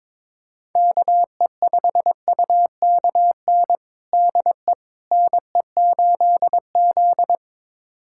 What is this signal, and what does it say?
Morse code, heard as a keyed tone, about 22 wpm, 700 Hz. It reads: KE5UKN DE NE8Z
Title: USA Callsign DE Callsign 22wpm